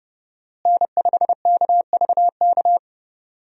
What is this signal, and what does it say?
Morse code, heard as a keyed tone, about 30 wpm, 700 Hz. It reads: N5KVK
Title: USA Call Signs - 5 Characters - Repeated 3x - 30wpm